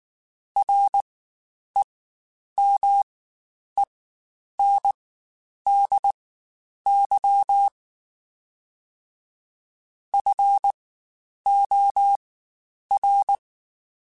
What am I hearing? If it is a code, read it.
Morse code, heard as a keyed tone, about 19 wpm, spaced as for 9 wpm, 800 Hz. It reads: REMENDY FOR
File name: Quotes Spaced / Quotes Spaced-mp3s / Temper sp @18